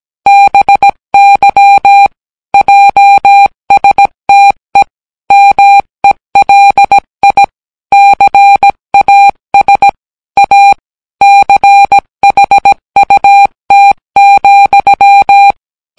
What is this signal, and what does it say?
Morse code, heard as a keyed tone, about 17 wpm, 800 Hz. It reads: BY JSTE MELI CAS A CHUT,